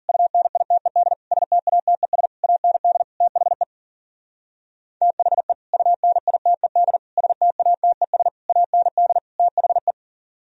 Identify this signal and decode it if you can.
Morse code, heard as a keyed tone, about 52 wpm, 700 Hz. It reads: UNITED STATES AND THE THE UNITED STATES AND THE